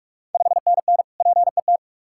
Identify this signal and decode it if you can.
Morse code, heard as a keyed tone, about 46 wpm, 700 Hz. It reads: 5NN PET
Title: Ontario QSO Party Exchange 45wpm